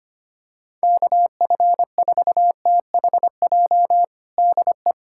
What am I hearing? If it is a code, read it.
Morse code, heard as a keyed tone, about 25 wpm, 700 Hz. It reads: KF4THJ DE
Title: USA Callsign DE Callsign 25wpm